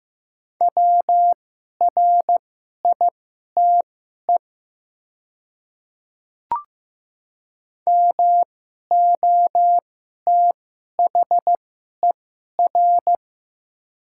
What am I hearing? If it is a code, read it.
Morse code, heard as a keyed tone, about 15 wpm, spaced as for 11 wpm, 700 Hz. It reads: WRITE MOTHER
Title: 2x Character Spacing - Top 200 Words - 15wpm